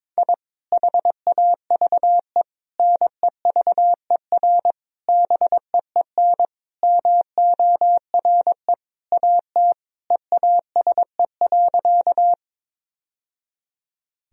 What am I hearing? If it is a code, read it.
Morse code, heard as a keyed tone, about 22 wpm, 700 Hz. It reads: I HAVE NEVER BEEN MORE AT EASE.
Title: Sentences from Top 1000 Words - Review Part 1 - 22wpm